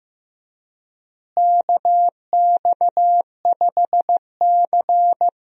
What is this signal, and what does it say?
Morse code, heard as a keyed tone, about 15 wpm, 700 Hz. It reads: KX5C
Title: USA Call Signs - 4 Characters - Repeated 2x - 15wpm